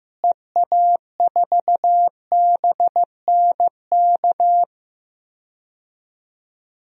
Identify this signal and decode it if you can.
Morse code, heard as a keyed tone, about 16 wpm, 700 Hz. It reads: EA4BNK
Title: DXCC-100 Call Signs - 6 Characters - Repeated 2x - 15wpm